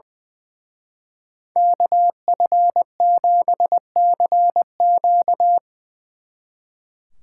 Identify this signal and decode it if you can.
Morse code, heard as a keyed tone, about 22 wpm, 700 Hz. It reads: KF7CQ
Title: Speed Racing - USA Call Signs 20wpm